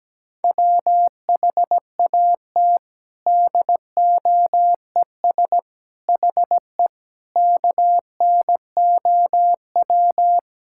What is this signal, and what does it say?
Morse code, heard as a keyed tone, about 17 wpm, 700 Hz. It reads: WHAT DOES HE KNOW
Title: Sentences from Top 1600 Words 17wpm